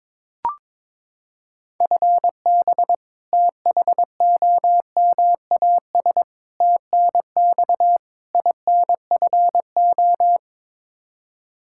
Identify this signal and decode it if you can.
Morse code, heard as a keyed tone, about 22 wpm, 700 Hz. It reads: FB THOMAS TNX INFO
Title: Practice QSO - Exchange 3 - More Info - 22wpm